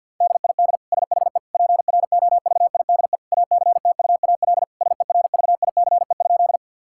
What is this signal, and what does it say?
Morse code, heard as a keyed tone, about 50 wpm, 700 Hz. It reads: DID SHE PROVIDE ACTUAL SERVICE?
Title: Sentences from Top 1000 Words - Review Part 5 - 50wpm